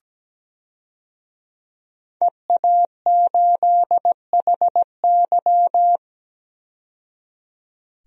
Morse code, heard as a keyed tone, about 17 wpm, 700 Hz. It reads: EA8HY
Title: DXCC-100 Call Signs - 5 Characters - 17wpm